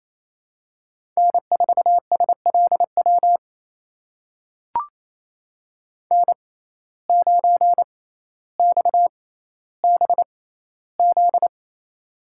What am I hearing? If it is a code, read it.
Morse code, heard as a keyed tone, about 28 wpm, spaced as for 14 wpm, 700 Hz. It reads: N4SLW N9XBZ
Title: USA Call Signs - 6x Character Spacing - 28wpm